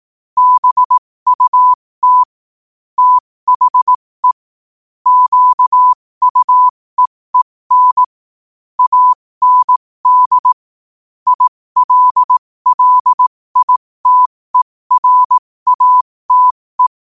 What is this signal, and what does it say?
Morse code, heard as a keyed tone, about 18 wpm, spaced as for 15 wpm, 1000 Hz. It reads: BUT THE QUEEN AND ILLITERATE
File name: qotdfiles / 15wpm